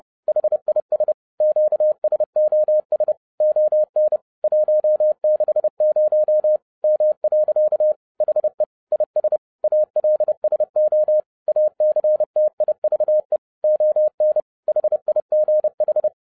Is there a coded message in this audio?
Yes: Morse code from 0.3 to 16.1 s, heard as a keyed tone, about 30 wpm, 610 Hz. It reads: HIS QSOS ON 160 M. HE IS ALSO ACTIVE ON HIGH